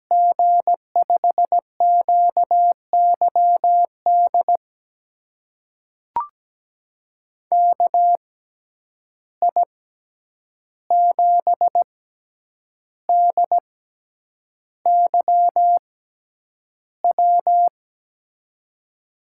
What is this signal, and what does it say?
Morse code, heard as a keyed tone, about 17 wpm, spaced as for 8 wpm, 700 Hz. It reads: G5QYD KI7DYW
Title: USA Call Signs - 6 Characters - 6x Character Spacing - 17wpm